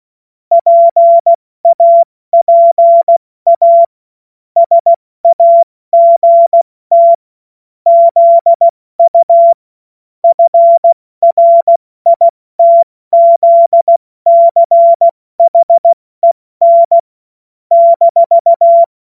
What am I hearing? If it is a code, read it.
Morse code, heard as a keyed tone, about 16 wpm, 680 Hz. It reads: PAPA SAGT ZU FRITZCHEN -
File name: Morsekurs / Witze / Witz 03. August 2021, 70 F80